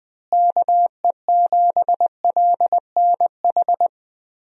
Morse code, heard as a keyed tone, about 20 wpm, 700 Hz. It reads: KE7LNH
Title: USA Call Signs - 6 Characters - Repeated 3x - 20wpm